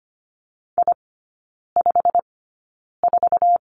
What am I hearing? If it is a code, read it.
Morse code, heard as a keyed tone, about 25 wpm, spaced as for 9 wpm, 700 Hz. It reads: I54
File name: Lesson 15e - Number 5 - Three Characters 25wpm